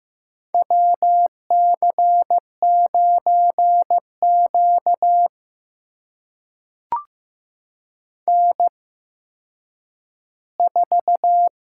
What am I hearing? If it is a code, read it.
Morse code, heard as a keyed tone, about 15 wpm, spaced as for 9 wpm, 700 Hz. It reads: WC9Q N4
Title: USA Call Signs - 8x Character Spacing - 15wpm